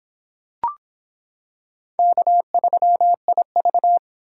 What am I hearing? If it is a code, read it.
Morse code, heard as a keyed tone, about 25 wpm, 700 Hz. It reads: K3IV